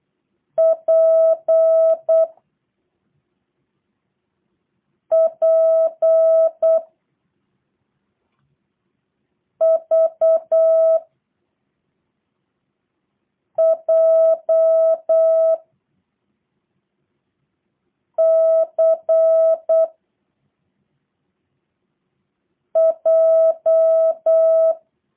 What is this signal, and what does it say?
Morse code, heard as a keyed tone, about 8 wpm, 650 Hz. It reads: P P V J C J
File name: jh8cbh / 2023cw / review1-5